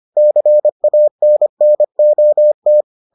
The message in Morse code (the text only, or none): CANNOT